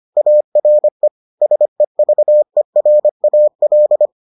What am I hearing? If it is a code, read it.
Morse code, heard as a keyed tone, about 25 wpm, 600 Hz. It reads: ARE SEVERAL